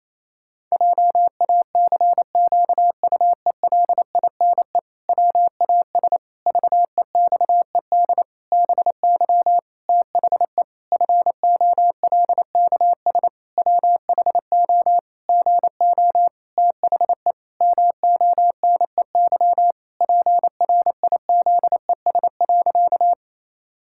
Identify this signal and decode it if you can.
Morse code, heard as a keyed tone, about 28 wpm, 700 Hz. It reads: JACQUELINE WAS VEXED BY THE FOLKS WHO GO THE MONEY PRIZES.
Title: Pangram Challenge v2 - 28wpm